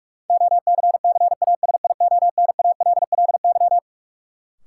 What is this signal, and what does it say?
Morse code, heard as a keyed tone, about 46 wpm, 700 Hz. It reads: OCCASIONALLY
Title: Sentences from Top 1300 Words - New Word Drill - 45wpm